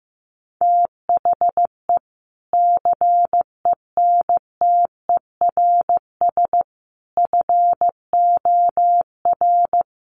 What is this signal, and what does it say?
Morse code, heard as a keyed tone, about 14 wpm, 700 Hz. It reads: THE CENTERS FOR
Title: Sets of 5 Words Encore2 15wpm